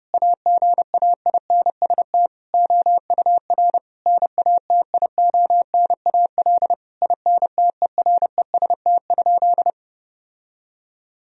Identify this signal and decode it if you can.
Morse code, heard as a keyed tone, about 30 wpm, 700 Hz. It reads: AGAINST OUR NATIONAL INTEREST?
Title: Sentences from Top 1000 Words - Review Part 4 - 30wpm